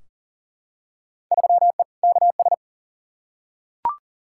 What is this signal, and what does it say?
Morse code, heard as a keyed tone, about 38 wpm, 700 Hz. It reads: 3E KS
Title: Field Day Exchanges Encore - 40wpm